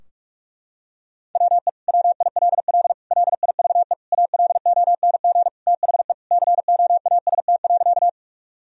Morse code, heard as a keyed tone, about 46 wpm, 700 Hz. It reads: WE WILL LIVE ALONG THE COAST.
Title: Sentences from Top 1000 Words - Review Part 4 - 45wpm